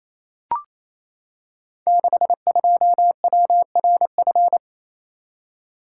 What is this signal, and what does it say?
Morse code, heard as a keyed tone, about 27 wpm, 700 Hz. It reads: 62WRF
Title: Code Groups - 5 Letter:Numbers - 28wpm